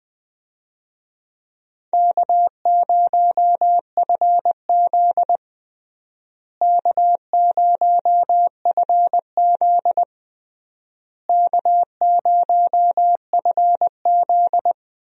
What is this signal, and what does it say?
Morse code, heard as a keyed tone, about 20 wpm, 700 Hz. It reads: K0FZ K0FZ K0FZ
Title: USA Call Signs - 4 Characters - Repeated 3x - 20wpm